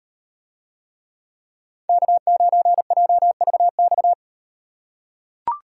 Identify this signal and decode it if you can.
Morse code, heard as a keyed tone, about 35 wpm, 700 Hz. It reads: K9JVX